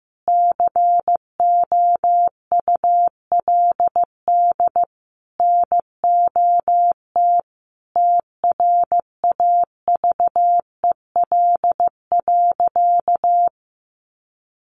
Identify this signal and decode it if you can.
Morse code, heard as a keyed tone, about 15 wpm, 700 Hz. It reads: COULD NOT TRAVEL.
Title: Sentences from Top 1300 Words 15wpm